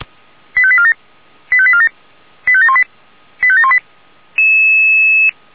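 Five-tone selective call (ZVEI), heard as quick runs of five tones, tones of about 70 ms. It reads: 85847 85847 85718 85718
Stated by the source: ZVEI five-tone call 89155, received by SDR.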